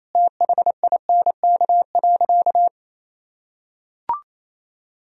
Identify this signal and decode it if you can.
Morse code, heard as a keyed tone, about 28 wpm, 700 Hz. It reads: THINK.
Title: Sentences from Top 1000 Words - Review Part 2 - 28wpm